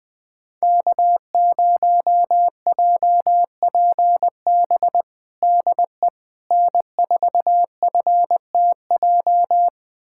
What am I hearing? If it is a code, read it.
Morse code, heard as a keyed tone, about 20 wpm, 700 Hz. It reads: K0JPB DE N4FTJ
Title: USA Callsign DE Callsign - 5 Characters - 20wpm